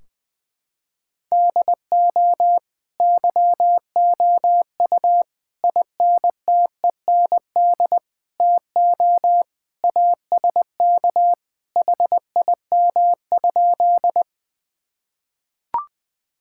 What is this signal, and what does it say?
Morse code, heard as a keyed tone, about 20 wpm, 700 Hz. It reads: DO YOU INTEND TO ASK HIM?
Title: Sentences from Top 1600 Words 20wpm